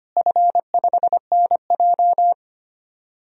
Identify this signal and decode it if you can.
Morse code, heard as a keyed tone, about 25 wpm, 700 Hz. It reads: F5NJ
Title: DXCC-100 Call Signs - 4 Characters - Repeated 3x - 25wpm